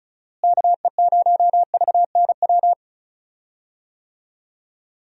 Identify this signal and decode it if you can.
Morse code, heard as a keyed tone, about 36 wpm, 700 Hz. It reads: KE0VNW
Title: USA Call Signs - 6 Characters - Repeated 3x - 35wpm